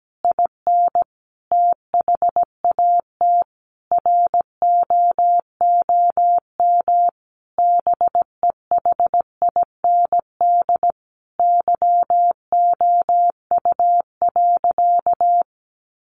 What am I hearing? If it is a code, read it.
Morse code, heard as a keyed tone, about 17 wpm, 700 Hz. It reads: IN THAT ROOM BEHIND YOU.
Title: Sentences from Top 500 Words 17wpm